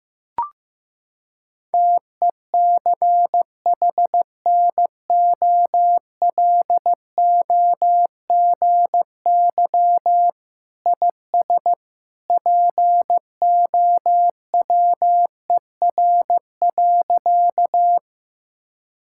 Morse code, heard as a keyed tone, about 15 wpm, 700 Hz. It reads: TECHNOLOGY IS POWER.
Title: Sentences from Top 1000 Words - Review Part 5 - 15wpm